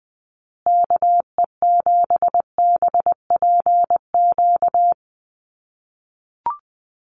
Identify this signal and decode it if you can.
Morse code, heard as a keyed tone, about 19 wpm, 700 Hz. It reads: KE7BPQ
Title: USA Call Signs Repeated 2x 20wpm